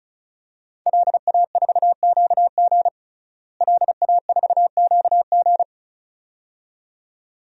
Morse code, heard as a keyed tone, about 36 wpm, 700 Hz. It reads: LA4QG LA4QG
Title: DXCC-100 Call Signs - 5 Characters - Repeated 2x - 35wpm